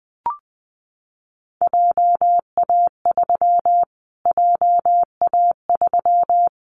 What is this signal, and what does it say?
Morse code, heard as a keyed tone, about 19 wpm, 700 Hz. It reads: JA3 JA3
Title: Callsign Prefixes for Top 25 DXCC - Repeated 3x - 20wpm